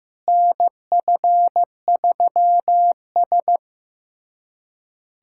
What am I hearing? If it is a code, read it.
Morse code, heard as a keyed tone, about 16 wpm, 700 Hz. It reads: NF3S
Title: USA Callsign DE Callsign - 4 Characters - 15wpm